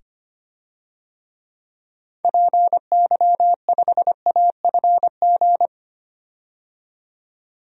Morse code, heard as a keyed tone, about 25 wpm, 690 Hz. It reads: PY5AFG
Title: DXCC-100 Call Signs - 6 Characters - 25wpm